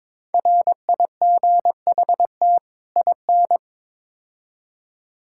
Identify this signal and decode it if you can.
Morse code, heard as a keyed tone, about 21 wpm, 700 Hz. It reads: RIGHT IN
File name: practice / 22wpm / Build A Phrase - 5-Word Phrases - 22wpm